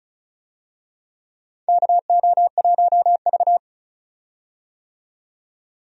Morse code, heard as a keyed tone, about 36 wpm, 700 Hz. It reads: KO1V
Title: USA Call Signs - 4 Characters - 35wpm